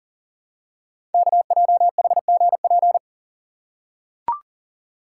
Morse code, heard as a keyed tone, about 42 wpm, 690 Hz. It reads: KJHGP
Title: Code Groups - 5 Letters - 40wpm